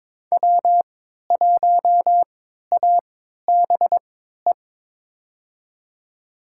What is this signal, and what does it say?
Morse code, heard as a keyed tone, about 22 wpm, spaced as for 12 wpm, 700 Hz. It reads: W1ABE
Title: USA Call Signs - 5 Characters - 3x Character Spacing - 22wpm